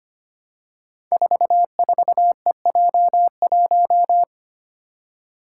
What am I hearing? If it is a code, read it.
Morse code, heard as a keyed tone, about 25 wpm, 700 Hz. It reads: 44EJ1
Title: Code Groups - 5 Letter:Numbers - 25wpm